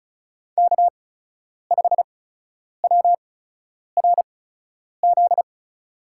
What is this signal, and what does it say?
Morse code, heard as a keyed tone, about 36 wpm, spaced as for 10 wpm, 700 Hz. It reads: K5WRZ
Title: USA Call Signs - 8x Character Spacing - 35wpm